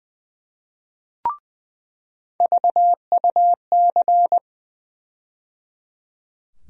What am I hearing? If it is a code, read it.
Morse code, heard as a keyed tone, about 20 wpm, 700 Hz. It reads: VUC
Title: USA Call Sign Postfixes - 3 Letters 20wpm